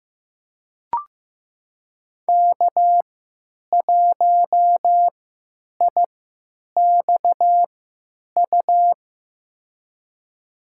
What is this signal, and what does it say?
Morse code, heard as a keyed tone, about 15 wpm, spaced as for 9 wpm, 700 Hz. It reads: K1IXU
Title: USA Call Signs - 3x Character Spacing - 15wpm